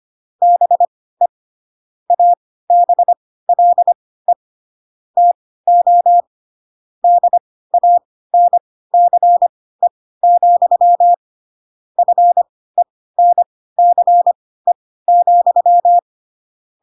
Morse code, heard as a keyed tone, about 25 wpm, spaced as for 16 wpm, 700 Hz. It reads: BE ABLE TO DANCE, FENCE,
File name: Pinocchio / Pinocchio ch2 10 25 ik1hge